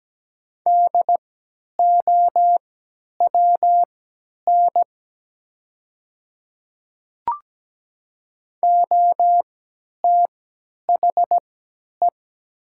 Morse code, heard as a keyed tone, about 17 wpm, spaced as for 10 wpm, 690 Hz. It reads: DOWN OTHE
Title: 3x Character Spacing - Top 100 Words - 17wpm